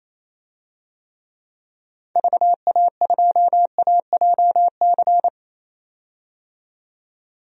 Morse code, heard as a keyed tone, about 28 wpm, 700 Hz. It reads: VA2AJC
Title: DXCC-100 Call Signs - 6 Characters - 28wpm